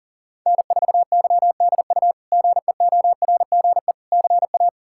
Morse code, heard as a keyed tone, about 40 wpm, 700 Hz. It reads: N4YDU GEORGE CA
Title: K1USN SST QSO Practice - 40wpm